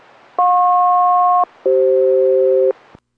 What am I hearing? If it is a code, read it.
Aircraft SELCAL, heard as two pairs of tones. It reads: JP-CF